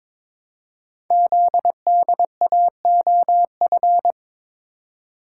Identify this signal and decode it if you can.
Morse code, heard as a keyed tone, about 22 wpm, 700 Hz. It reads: ZDAOF
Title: Code Groups - 5 Letter:Numbers - 22wpm